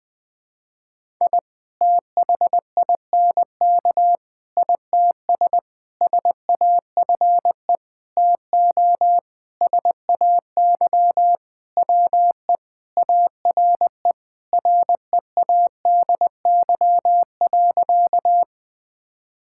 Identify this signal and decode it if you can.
Morse code, heard as a keyed tone, about 19 wpm, 700 Hz. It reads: I THINK ITS SAFE TO SAY WE ARE READY.
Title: Sentences with 6-Word Phrases - 20wpm